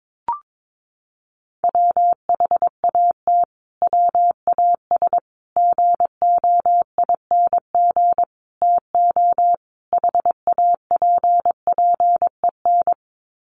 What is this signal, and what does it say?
Morse code, heard as a keyed tone, about 22 wpm, 700 Hz. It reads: WHAT WAS GOING TO HAPPEN
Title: Sets of 5 Words Encore 22wpm